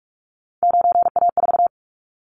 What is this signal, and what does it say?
Morse code, heard as a keyed tone, about 44 wpm, 700 Hz. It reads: 9A4
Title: Callsign Prefixes for Top 50 DXCC - 45wpm